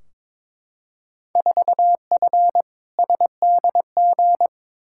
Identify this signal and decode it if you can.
Morse code, heard as a keyed tone, about 22 wpm, 700 Hz. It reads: 4F SDG
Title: Field Day Exchanges - 22wpm